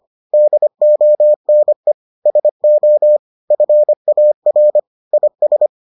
Morse code, heard as a keyed tone, about 25 wpm, 600 Hz. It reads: DONE SO FAR IS